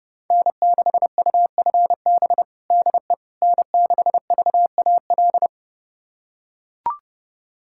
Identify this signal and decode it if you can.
Morse code, heard as a keyed tone, about 32 wpm, 700 Hz. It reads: N6UFB DE N6VAL
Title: USA Callsign DE Callsign - 5 Characters - 30wpm